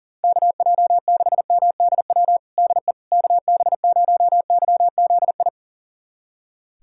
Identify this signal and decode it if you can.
Morse code, heard as a keyed tone, about 40 wpm, 700 Hz. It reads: KJ6MDW DE KB0YZI
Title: USA Callsign DE Callsign 40wpm